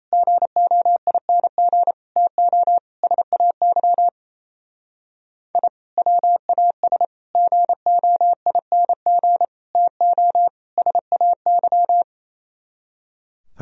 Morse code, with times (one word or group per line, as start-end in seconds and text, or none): 0.1-1.9 GOING
2.2-2.8 TO
3.0-4.1 SAY
5.5-5.7 I
6.0-7.0 WAS
7.3-9.4 GOING
9.8-10.5 TO
10.8-12.0 SAY